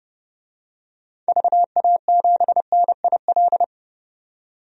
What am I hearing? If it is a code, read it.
Morse code, heard as a keyed tone, about 30 wpm, 700 Hz. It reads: VA7NIL
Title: DXCC-100 Call Signs - 6 Characters - 30wpm